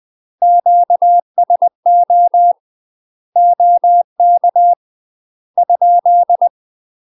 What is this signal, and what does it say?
Morse code, heard as a keyed tone, about 20 wpm, 700 Hz. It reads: QSO OK ?
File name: UK MASTER QSOs TRAINING / 20WPM / UK MASTER QSOs 0053